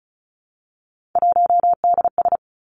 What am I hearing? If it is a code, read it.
Morse code, heard as a keyed tone, about 36 wpm, 700 Hz. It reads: 1DS
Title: Three Challenging Letter-Number Combinations 35wpm